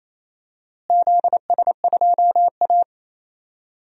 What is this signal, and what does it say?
Morse code, heard as a keyed tone, about 28 wpm, 700 Hz. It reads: ZS2A
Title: DXCC-100 Call Signs - 4 Characters - Repeated 2x - 28wpm